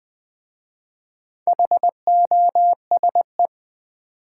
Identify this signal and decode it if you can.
Morse code, heard as a keyed tone, about 20 wpm, 700 Hz. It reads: HOSE
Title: Easy Words with Hints - 20wpm